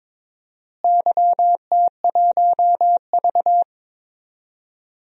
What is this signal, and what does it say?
Morse code, heard as a keyed tone, about 22 wpm, 700 Hz. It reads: YT1V
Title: DXCC-100 Call Signs - 4 Characters - Repeated 2x - 22wpm